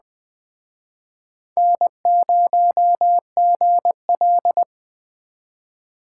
Morse code, heard as a keyed tone, about 20 wpm, 700 Hz. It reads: N0GL
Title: Speed Racing - USA 1x2 Call Signs 17wpm